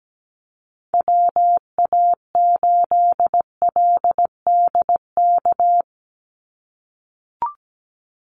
Morse code, heard as a keyed tone, about 17 wpm, 700 Hz. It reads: WA8LDK
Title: USA Call Signs 17wpm